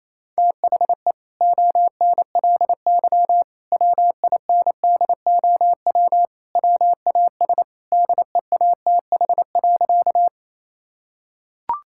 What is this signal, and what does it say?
Morse code, heard as a keyed tone, about 28 wpm, 700 Hz. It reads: THE ONLY WINDOW WAS DEATH.
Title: Sentences from Top 1600 Words 28wpm